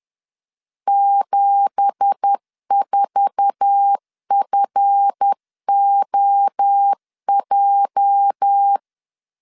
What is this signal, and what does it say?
Morse code, heard as a keyed tone, about 10 wpm, 800 Hz. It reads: 74FOJ
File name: LEZIONI / 9 2867 10wpm 2733